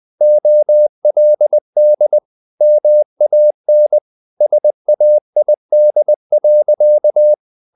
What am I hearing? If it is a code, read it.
Morse code, heard as a keyed tone, about 20 wpm, 600 Hz. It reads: OLD MAN SAID.
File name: the old man and the sea part 2 20wpm